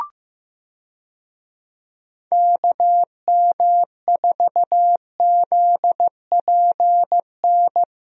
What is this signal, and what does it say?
Morse code, heard as a keyed tone, about 15 wpm, 700 Hz. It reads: KM4ZPN
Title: USA Call Signs - 6 Characters - Repeated 2x - 15wpm